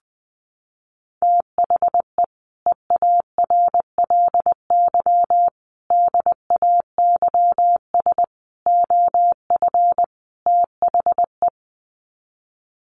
Morse code, heard as a keyed tone, about 20 wpm, 700 Hz. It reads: THE EARLY DAYS OF THE
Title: Sets of 5 Words Encore 20wpm